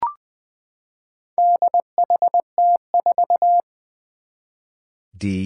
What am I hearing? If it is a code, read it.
Morse code, heard as a keyed tone, about 21 wpm, 700 Hz. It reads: DHT4
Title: Four Challenging Letter-Number Combinations 20wpm